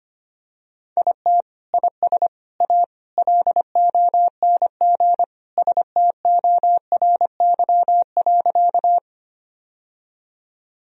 Morse code, heard as a keyed tone, about 25 wpm, 700 Hz. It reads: IT IS A LONG STORY.
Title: Sentences from Top 1000 Words - Review Part 1 - 25wpm